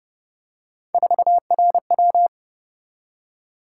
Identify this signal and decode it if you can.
Morse code, heard as a keyed tone, about 32 wpm, 700 Hz. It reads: 4RW